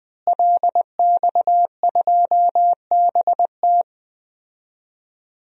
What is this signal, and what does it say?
Morse code, heard as a keyed tone, about 20 wpm, 700 Hz. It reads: LX2BT
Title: DXCC-100 Call Signs - 5 Characters - Repeated 2x - 20wpm